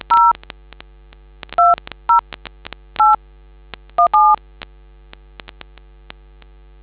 DTMF, heard as telephone touch tones, tones of about 180 ms. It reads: 0 20 8 17